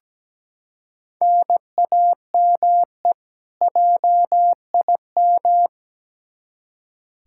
Morse code, heard as a keyed tone, about 17 wpm, 700 Hz. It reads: NAME JIM
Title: K1USN SST QSO Practice 17wpm